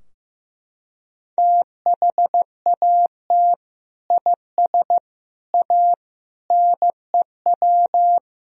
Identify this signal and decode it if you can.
Morse code, heard as a keyed tone, about 15 wpm, 700 Hz. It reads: THAT IS A NEW
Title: Sentences from Top 1000 Words - Review Part 3 - 15wpm